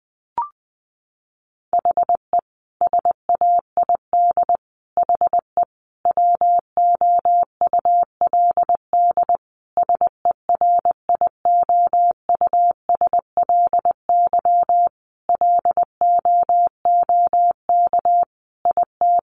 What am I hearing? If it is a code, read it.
Morse code, heard as a keyed tone, about 20 wpm, 700 Hz. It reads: HE SAID HE WOULD SERIOUSLY LOOK IT
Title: Sentences from Top 1200 Words 20wpm